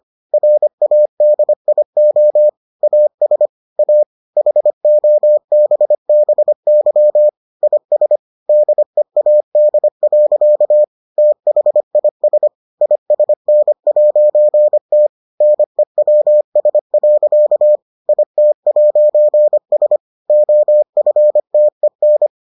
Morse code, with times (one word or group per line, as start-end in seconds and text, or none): 0.3-2.5 RADIO
2.8-3.5 AS
3.8-4.0 A
4.4-7.3 HOBBY
7.6-8.2 IS
8.5-10.8 DEAD.
11.2-12.5 THIS
12.8-15.1 ISN'T
15.4-17.8 NEWS.
18.1-20.0 IT'S
20.3-22.3 OFTEN